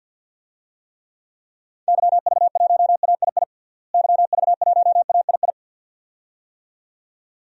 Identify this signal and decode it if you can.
Morse code, heard as a keyed tone, about 51 wpm, 700 Hz. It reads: YV1AII YV1AII
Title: DXCC-100 Call Signs - 6 Characters - Repeated 2x - 50wpm